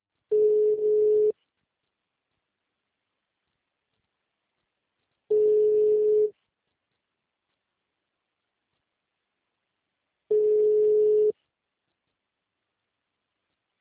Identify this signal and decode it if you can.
Ringback tone (Europe, 425 Hz).